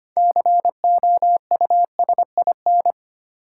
Morse code, heard as a keyed tone, about 25 wpm, 700 Hz. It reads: COUSIN
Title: Six Letter Words - 25wpm